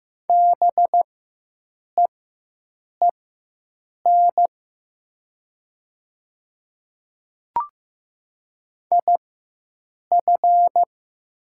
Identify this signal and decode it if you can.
Morse code, heard as a keyed tone, about 15 wpm, spaced as for 7 wpm, 700 Hz. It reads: BEEN IF
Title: 4x Character Spacing - Top 100 Words - 15wpm